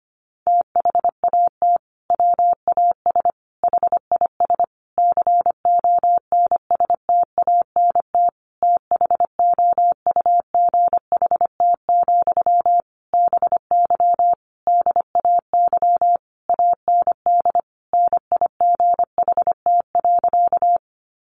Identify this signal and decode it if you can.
Morse code, heard as a keyed tone, about 25 wpm, 700 Hz. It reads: THAT WAS HIS CONSTANT THOUGHT, BY DAY AND NIGHT.
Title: Sentences from Top 1500 Words 25wpm